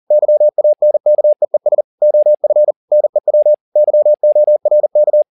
Morse code, heard as a keyed tone, about 40 wpm, 600 Hz. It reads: YANKEES OF NEW YORK